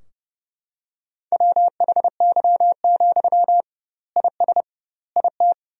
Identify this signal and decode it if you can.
Morse code, heard as a keyed tone, about 32 wpm, 700 Hz. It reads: WHY, IS IT
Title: 2x Word-Spacing - Sentences from Top 300 Words - 30wpm